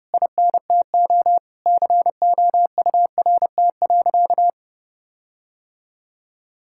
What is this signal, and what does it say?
Morse code, heard as a keyed tone, about 28 wpm, 700 Hz. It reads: INTO COURT.
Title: Sentences from Top 1600 Words 30wpm